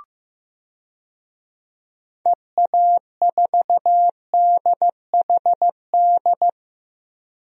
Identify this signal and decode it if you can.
Morse code, heard as a keyed tone, about 15 wpm, 700 Hz. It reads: EA4DHD
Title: DXCC-100 Call Signs - 6 Characters - Repeated 2x - 15wpm